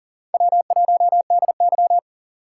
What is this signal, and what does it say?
Morse code, heard as a keyed tone, about 37 wpm, 700 Hz. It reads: W1DY
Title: USA Call Signs - 6x Character Spacing - 40wpm